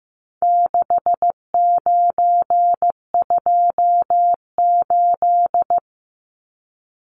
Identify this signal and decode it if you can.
Morse code, heard as a keyed tone, about 15 wpm, 700 Hz. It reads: 6928
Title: Numbers - 4-Digit 15wpm